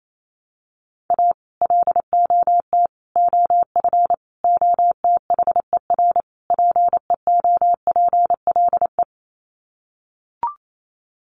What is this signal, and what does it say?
Morse code, heard as a keyed tone, about 28 wpm, 700 Hz. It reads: A LOT OF OTHER PEOPLE
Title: Sets of 5 Words Encore2 28wpm